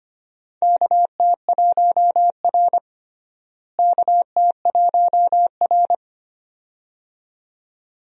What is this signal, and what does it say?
Morse code, heard as a keyed tone, about 25 wpm, 700 Hz. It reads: KT1R KT1R
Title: USA Call Signs - 4 Characters - Repeated 2x - 25wpm